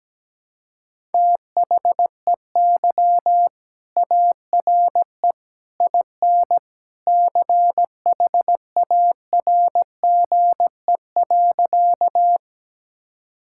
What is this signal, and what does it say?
Morse code, heard as a keyed tone, about 17 wpm, 700 Hz. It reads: THEY ARE IN CHARGE.
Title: Sentences from Top 1000 Words - Review Part 6 - 17wpm